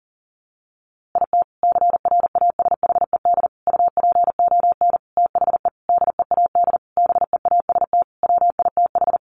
Morse code, heard as a keyed tone, about 40 wpm, 700 Hz. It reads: IT CRASHED UPON THE DEAD BEAST WITH
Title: A Sound Of Thunder 40wpm